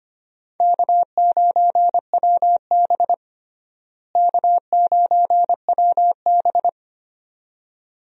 Morse code, heard as a keyed tone, about 25 wpm, 700 Hz. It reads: K9WB K9WB
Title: USA Call Signs - 4 Characters - Repeated 2x - 25wpm